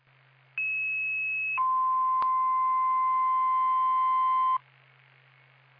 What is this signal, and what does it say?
Two-tone page, heard as a one-second tone, then a three-second tone, about 2600 then 1040 Hz. It is heard from 0.6 to 4.6 s.